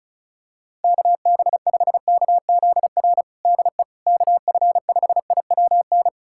Morse code, heard as a keyed tone, about 34 wpm, 690 Hz. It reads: KB5KZR DE KF5IWN
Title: USA Callsign DE Callsign - 6 Characters - 35wpm